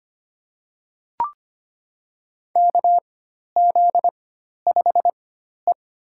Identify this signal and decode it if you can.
Morse code, heard as a keyed tone, about 25 wpm, spaced as for 12 wpm, 700 Hz. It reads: KZ5E
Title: USA Call Signs - 4 Characters - 4x Character Spacing - 25wpm